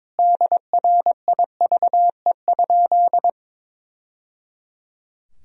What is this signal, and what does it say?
Morse code, heard as a keyed tone, about 21 wpm, 700 Hz. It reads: DRIVE?
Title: Sentences from Top 1000 Words - Review Part 2 - 22wpm